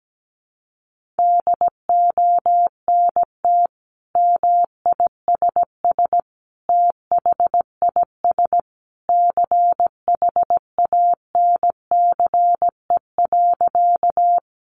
Morse code, heard as a keyed tone, about 17 wpm, 700 Hz. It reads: DONT MISS THIS CHANCE.
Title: Sentences from Top 800 Words 17wpm